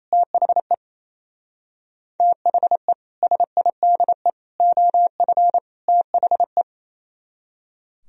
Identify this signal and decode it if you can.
Morse code, heard as a keyed tone, about 37 wpm, 700 Hz. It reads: THE THE SIDE OF THE